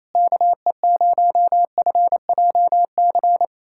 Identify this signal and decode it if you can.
Morse code, heard as a keyed tone, about 28 wpm, 700 Hz. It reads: KE0FJC
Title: USA Callsign DE Callsign - 6 Characters - 28wpm